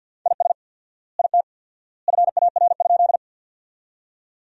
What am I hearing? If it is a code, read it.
Morse code, heard as a keyed tone, about 50 wpm, 700 Hz. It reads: IS IT FAR?
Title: 4x Word-Spacing - Sentences from Top 300 Words - 50wpm